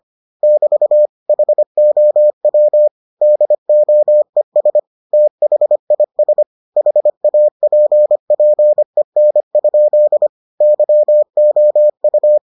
Morse code, heard as a keyed tone, about 25 wpm, 600 Hz. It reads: <BT> HOW DOES THIS HAPPEN? YOU